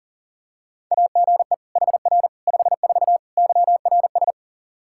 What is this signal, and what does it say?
Morse code, heard as a keyed tone, about 40 wpm, 700 Hz. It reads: AGE HR 54 YRS
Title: Practice QSO - Exchange 3 - More Info - 40wpm